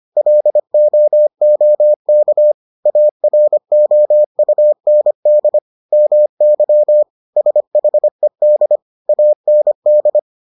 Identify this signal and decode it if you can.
Morse code, heard as a keyed tone, about 25 wpm, 600 Hz. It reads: LOOK AROUND MY SHED AND